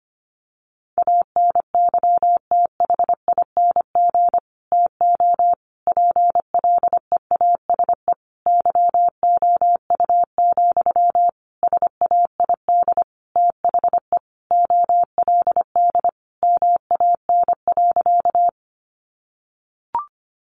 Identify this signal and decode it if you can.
Morse code, heard as a keyed tone, about 25 wpm, 700 Hz. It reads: ANYTHING TO PLEASE YOU, SAID THE OLD MAN.
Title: Sentences from Top 1500 Words 25wpm